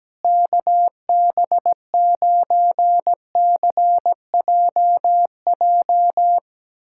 Morse code, heard as a keyed tone, about 17 wpm, 700 Hz. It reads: KB9CJJ
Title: USA Callsign DE Callsign - 6 Characters - 17wpm